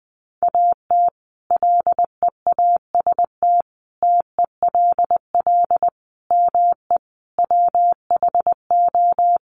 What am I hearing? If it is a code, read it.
Morse code, heard as a keyed tone, about 20 wpm, 700 Hz. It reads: AT LEAST TELL ME WHO
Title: Sentences from Top 700 Words 20wpm